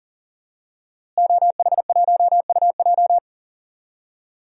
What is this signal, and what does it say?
Morse code, heard as a keyed tone, about 40 wpm, 700 Hz. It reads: OH1UJ
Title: DXCC-100 Call Signs - 5 Characters - Repeated 3x - 40wpm